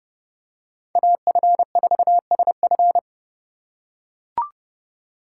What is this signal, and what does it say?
Morse code, heard as a keyed tone, about 30 wpm, 700 Hz. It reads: AF4SF
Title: USA Call Signs - 5 Characters - 4x Character Spacing - 30wpm